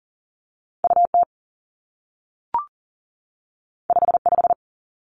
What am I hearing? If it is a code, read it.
Morse code, heard as a keyed tone, about 40 wpm, 700 Hz. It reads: U T 5 5
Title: Two Challenging Letter-Number Combinations 40wpm